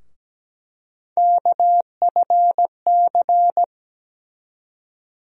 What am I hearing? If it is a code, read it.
Morse code, heard as a keyed tone, about 17 wpm, 700 Hz. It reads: KFC
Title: USA Call Sign Postfixes - 3 Letters 17wpm